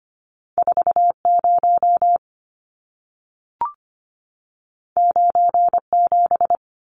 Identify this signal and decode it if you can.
Morse code, heard as a keyed tone, about 25 wpm, 700 Hz. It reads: 40 97
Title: Numbers - 2-Digit 25wpm